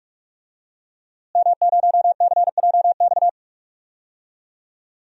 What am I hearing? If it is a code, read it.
Morse code, heard as a keyed tone, about 46 wpm, 700 Hz. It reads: M0CJX
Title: DXCC-100 Call Signs - 5 Characters - Repeated 3x - 45wpm